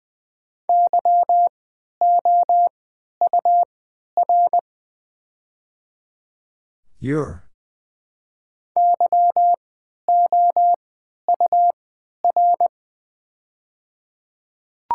Morse code, heard as a keyed tone, about 20 wpm, spaced as for 11 wpm, 700 Hz. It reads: YOUR YOUR
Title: 3x Character Spacing - Top 100 Words - 20wpm